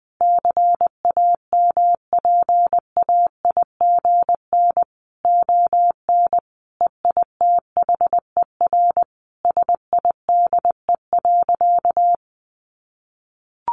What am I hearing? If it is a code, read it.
Morse code, heard as a keyed tone, about 20 wpm, 700 Hz. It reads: CAMPAIGN ON EITHER SIDE.
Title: Sentences from Top 1200 Words 20wpm